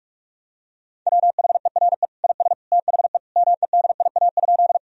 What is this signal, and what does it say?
Morse code, heard as a keyed tone, about 47 wpm, 700 Hz. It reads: WHERE IS THE MEDIA?
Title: Sentences from Top 1000 Words - Review Part 4 - 45wpm